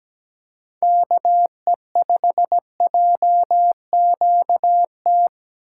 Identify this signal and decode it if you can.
Morse code, heard as a keyed tone, about 17 wpm, 700 Hz. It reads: KE5JQT
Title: USA Call Signs - 4x Character Spacing - 17wpm